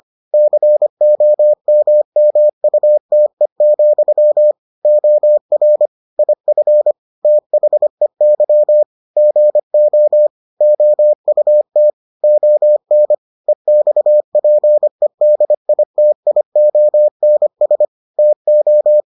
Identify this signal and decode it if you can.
Morse code, heard as a keyed tone, about 25 wpm, 600 Hz. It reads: COMMUTE, OR IF THEY GO OUT ON EXPEDITIONS TO